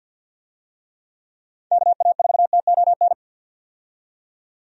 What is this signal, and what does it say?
Morse code, heard as a keyed tone, about 49 wpm, 700 Hz. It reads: KA4TON
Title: USA Call Signs - 6 Characters - 50wpm